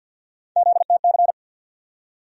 Morse code, heard as a keyed tone, about 47 wpm, 700 Hz. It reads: ZTC
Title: USA Call Sign Postfixes - 3 Letters 50wpm